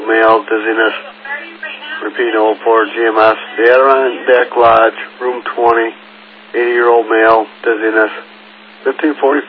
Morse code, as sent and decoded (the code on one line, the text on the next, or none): none